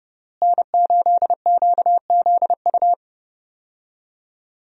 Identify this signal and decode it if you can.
Morse code, heard as a keyed tone, about 32 wpm, 700 Hz. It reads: N8QZU
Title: USA Call Signs - 5 Characters - Repeated 3x - 30wpm